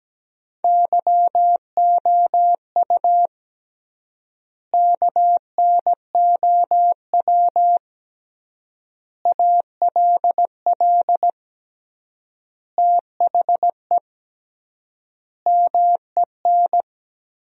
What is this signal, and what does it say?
Morse code, heard as a keyed tone, about 17 wpm, 700 Hz. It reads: YOU KNOW ALL THE MEN
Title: 3x Word-Spacing - Sentences from Top 300 Words - 17wpm